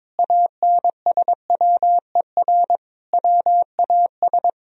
Morse code, heard as a keyed tone, about 22 wpm, 700 Hz. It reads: ANSWER WAS
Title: Sentences from Top 1000 Words - Review Part 2 - 22wpm